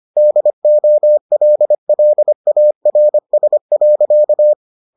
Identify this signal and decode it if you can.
Morse code, heard as a keyed tone, about 25 wpm, 600 Hz. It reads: DOLLARS.